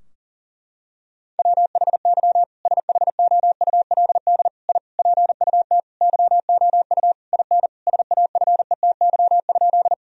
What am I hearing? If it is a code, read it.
Morse code, heard as a keyed tone, about 40 wpm, 700 Hz. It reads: WHY SHOULD I PUT YOU IN SAFETY?
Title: Sentences from Top 1000 Words - Review Part 4 - 40wpm